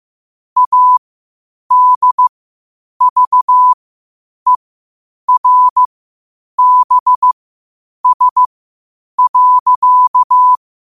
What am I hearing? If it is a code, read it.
Morse code, heard as a keyed tone, about 15 wpm, spaced as for 8 wpm, 1000 Hz. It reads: ADVERBS.